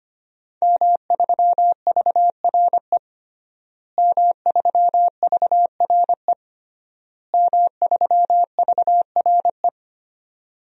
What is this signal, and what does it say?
Morse code, heard as a keyed tone, about 25 wpm, 700 Hz. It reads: M3VRE M3VRE M3VRE
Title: DXCC-100 Call Signs - 5 Characters - Repeated 3x - 25wpm